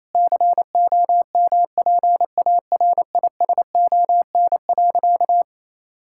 Morse code, heard as a keyed tone, about 27 wpm, 700 Hz. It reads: COMPARISON.
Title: Sentences from Top 1600 Words 28wpm